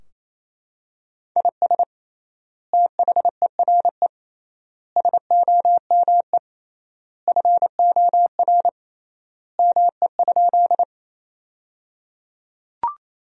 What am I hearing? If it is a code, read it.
Morse code, heard as a keyed tone, about 28 wpm, 700 Hz. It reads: IS THERE SOME FOR ME?
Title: 3x Word-Spacing - Sentences from Top 200 Words - 28wpm